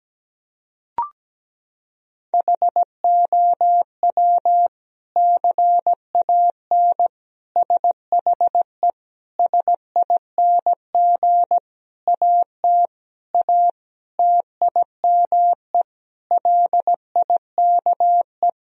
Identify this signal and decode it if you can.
Morse code, heard as a keyed tone, about 17 wpm, 700 Hz. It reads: HOW CAN SHE SING AT A TIME LIKE
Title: Sentences from Top 1000 Words - Review Part 2 - 17wpm